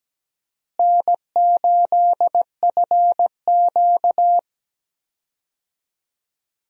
Morse code, heard as a keyed tone, about 17 wpm, 700 Hz. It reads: N8FQ
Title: USA Call Signs - 4 Characters - Repeated 3x - 17wpm